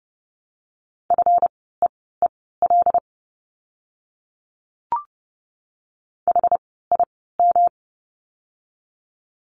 Morse code, heard as a keyed tone, about 30 wpm, spaced as for 17 wpm, 700 Hz. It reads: FEEL HIM
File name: course / 30:10wpm / 237 Lesson 38f - Letter q - Words 30wpm